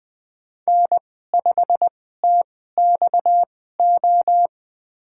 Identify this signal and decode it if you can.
Morse code, heard as a keyed tone, about 20 wpm, spaced as for 14 wpm, 700 Hz. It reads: N5TXO
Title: USA Call Signs - 5 Characters - 2x Character Spacing - 20wpm